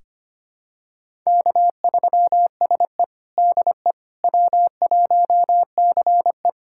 Morse code, heard as a keyed tone, about 25 wpm, 700 Hz. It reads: K3SE DE W1CE
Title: USA Callsign DE Callsign - 4 Characters - 25wpm